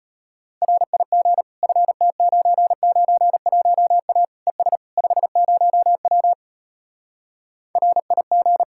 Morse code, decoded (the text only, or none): RIG FT991A ES 50W RIG